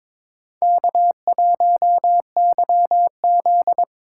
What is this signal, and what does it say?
Morse code, heard as a keyed tone, about 26 wpm, 700 Hz. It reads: K1YZ
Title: Speed Racing - USA 1x2 Call Signs 22wpm